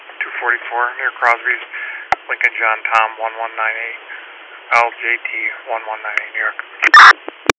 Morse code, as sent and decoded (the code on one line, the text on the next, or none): none